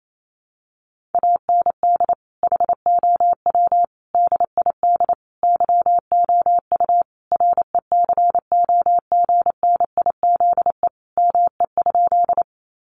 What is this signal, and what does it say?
Morse code, heard as a keyed tone, about 28 wpm, 700 Hz. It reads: AND HOW DID YOU RECOGNIZE ME?
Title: Sentences from Top 1400 Words 28wpm